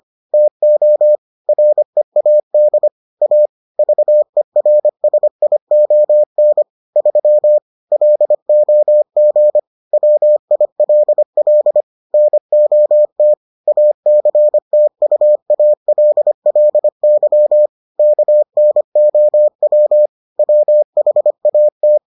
Morse code, heard as a keyed tone, about 25 wpm, 600 Hz. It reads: TO READ A VERSION 3 LOG WILL NOT ACTUALLY KNOW WHAT